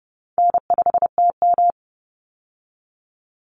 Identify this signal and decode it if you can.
Morse code, heard as a keyed tone, about 30 wpm, 700 Hz. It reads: N5TM
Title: Call Signs Repeated 3x - 30wpm